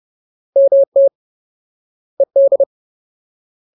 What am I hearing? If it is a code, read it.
Morse code, heard as a keyed tone, about 30 wpm, 550 Hz. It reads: MT ED
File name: Ultimate Word List / 30WPM Ultimate